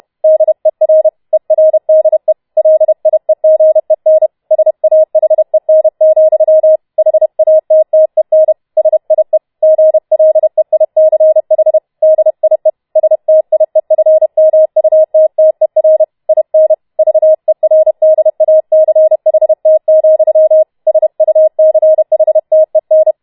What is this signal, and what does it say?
Morse code, heard as a keyed tone, about 31 wpm, 630 Hz. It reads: DER ERDE LIEGEN SAHEN, HATTEN SIE GLEICH DIE STIEFMUTTER IN VERDACHT, SUCHTEN